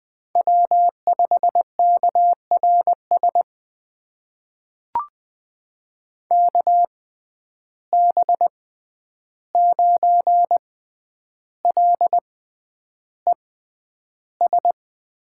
Morse code, heard as a keyed tone, about 20 wpm, spaced as for 10 wpm, 700 Hz. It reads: W5KRS KB9LES
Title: USA Call Signs - 6x Character Spacing - 20wpm